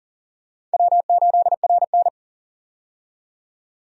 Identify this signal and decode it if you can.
Morse code, heard as a keyed tone, about 40 wpm, 700 Hz. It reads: W8RN
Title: USA Call Signs - 4 Characters - 40wpm